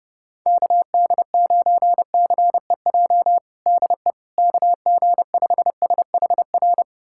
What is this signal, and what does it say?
Morse code, heard as a keyed tone, about 30 wpm, 700 Hz. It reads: KD9CEJ DE KG5SHR
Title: USA Callsign DE Callsign - 6 Characters - 30wpm